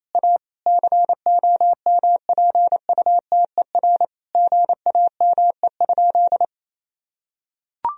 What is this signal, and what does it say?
Morse code, heard as a keyed tone, about 28 wpm, 700 Hz. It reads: A COMPUTER GAME?
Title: Sentences from Top 1000 Words - Review Part 3 - 28wpm